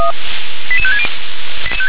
Five-tone selective call (ZVEI), heard as quick runs of five tones, tones of about 70 ms. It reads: 88560